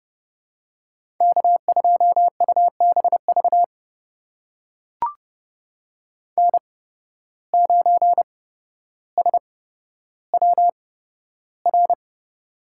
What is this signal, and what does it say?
Morse code, heard as a keyed tone, about 30 wpm, spaced as for 13 wpm, 700 Hz. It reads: K2UBV N9SWR
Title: USA Call Signs - 5 Characters - 8x Character Spacing - 30wpm